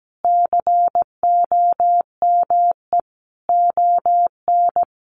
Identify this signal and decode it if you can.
Morse code, heard as a keyed tone, about 17 wpm, 700 Hz. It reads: COME ON
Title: Sentences from Top 700 Words 17wpm